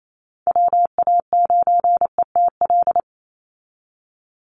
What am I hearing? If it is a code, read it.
Morse code, heard as a keyed tone, about 28 wpm, 700 Hz. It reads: WA9ETL
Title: USA Call Signs 28wpm